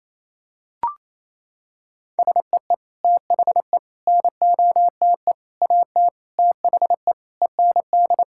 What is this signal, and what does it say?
Morse code, heard as a keyed tone, about 28 wpm, 700 Hz. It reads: SEE THE NOTE AT THE END
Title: Sentences from Top 1000 Words - Review Part 5 - 28wpm